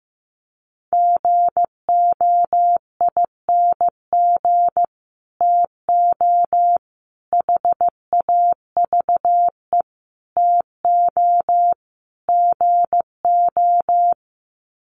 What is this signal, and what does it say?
Morse code, heard as a keyed tone, about 15 wpm, 700 Hz. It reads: GOING TO HAVE TO GO
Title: Sets of 5 Words Encore 15wpm